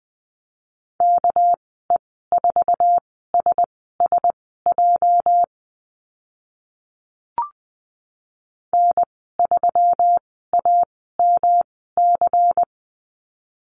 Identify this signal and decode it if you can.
Morse code, heard as a keyed tone, about 20 wpm, spaced as for 14 wpm, 700 Hz. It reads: KE4SSJ N3AMC
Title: 245 Lesson 39g - Letter z - Call Signs 20wpm